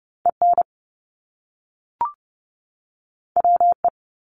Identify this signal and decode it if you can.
Morse code, heard as a keyed tone, about 30 wpm, 700 Hz. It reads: EN WE
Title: Top Two Letter Combinations in Words 30wpm